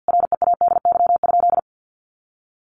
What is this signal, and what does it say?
Morse code, heard as a keyed tone, about 50 wpm, 700 Hz. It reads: READY?
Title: Sentences from Top 1400 Words 50wpm